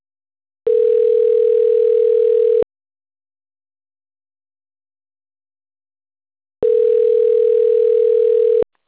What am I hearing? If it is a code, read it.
Ringback tone (North America), 440 + 480 Hz.
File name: autotestagent / 11-29-03.136..callcentertransfer. .6cc81d8d-aa9f-43d5-9403